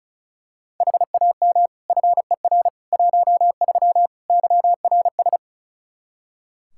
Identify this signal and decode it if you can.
Morse code, heard as a keyed tone, about 36 wpm, 700 Hz. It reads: HAM FER 13 YRS
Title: Practice QSO - Exchange 3 - More Info - 35wpm